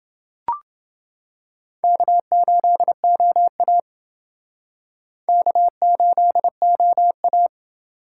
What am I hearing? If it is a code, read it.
Morse code, heard as a keyed tone, about 29 wpm, 700 Hz. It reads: K8OA K8OA